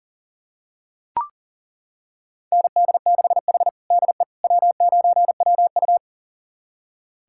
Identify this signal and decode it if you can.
Morse code, heard as a keyed tone, about 40 wpm, 700 Hz. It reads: ND6H DE W9WU
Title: USA Callsign DE Callsign - 4 Characters - 40wpm